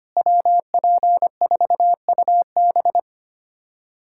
Morse code, heard as a keyed tone, about 25 wpm, 700 Hz. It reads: WP4UB
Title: USA Callsign DE Callsign - 5 Characters - 25wpm